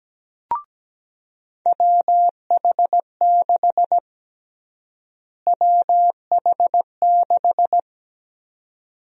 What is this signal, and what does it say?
Morse code, heard as a keyed tone, about 17 wpm, 700 Hz. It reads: WH6 WH6
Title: USA Call Sign Prefixes - Repeated 2x - 17wpm